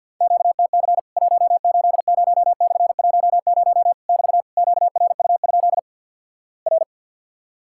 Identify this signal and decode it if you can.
Morse code, heard as a keyed tone, about 50 wpm, 700 Hz. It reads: QTC 180/10 <BT> QRU? R
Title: Worked All Europe DX Contest - QTC Exchange - 50wpm